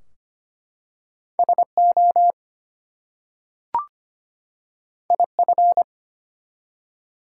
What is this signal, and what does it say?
Morse code, heard as a keyed tone, about 25 wpm, 700 Hz. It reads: SO IF
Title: Two Letter Words - 25wpm